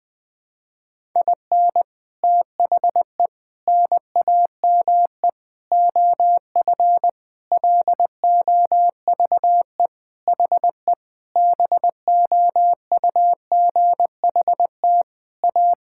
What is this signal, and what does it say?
Morse code, heard as a keyed tone, about 20 wpm, 700 Hz. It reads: IN THE NAME OF LOVE HE BOUGHT A
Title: Sentences with 4-Word Phrases - 20wpm